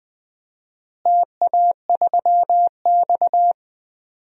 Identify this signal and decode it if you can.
Morse code, heard as a keyed tone, about 20 wpm, 700 Hz. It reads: TA3X
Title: DXCC-100 Call Signs - 4 Characters - Repeated 3x - 20wpm